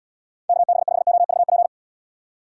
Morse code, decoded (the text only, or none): KB5OVP